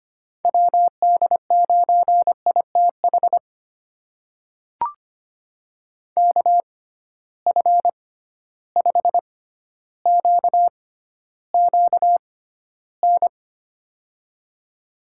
Morse code, heard as a keyed tone, about 25 wpm, spaced as for 13 wpm, 700 Hz. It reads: WD9ITH KF5QQN
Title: USA Call Signs - 6 Characters - 6x Character Spacing - 25wpm